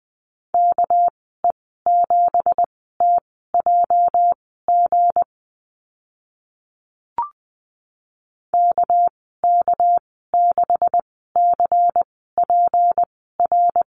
Morse code, heard as a keyed tone, about 20 wpm, spaced as for 14 wpm, 700 Hz. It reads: KE7TJG KK6CPR
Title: 219 Lesson 35g - Number 0 - Call Signs 20wpm